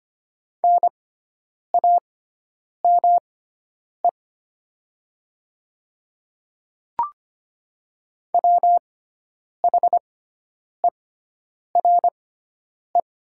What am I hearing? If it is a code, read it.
Morse code, heard as a keyed tone, about 25 wpm, spaced as for 9 wpm, 700 Hz. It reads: NAME WHERE
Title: 6x Character Spacing - Top 200 Words - 25wpm